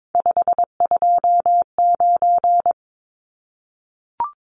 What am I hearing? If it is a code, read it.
Morse code, heard as a keyed tone, about 22 wpm, 700 Hz. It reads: 529
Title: Numbers - 3-Digit 22wpm